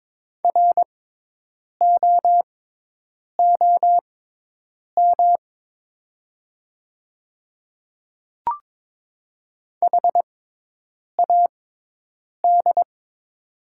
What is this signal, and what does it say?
Morse code, heard as a keyed tone, about 22 wpm, spaced as for 8 wpm, 700 Hz. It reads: ROOM HAD
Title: 6x Character Spacing - Top 300 Words - 22wpm